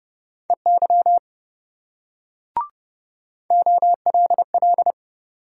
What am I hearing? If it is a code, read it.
Morse code, heard as a keyed tone, about 26 wpm, 700 Hz. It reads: EY OLL